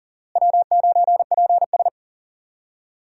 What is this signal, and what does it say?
Morse code, heard as a keyed tone, about 37 wpm, 700 Hz. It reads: W9PS
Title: USA Call Signs - 4 Characters - Repeated 3x - 40wpm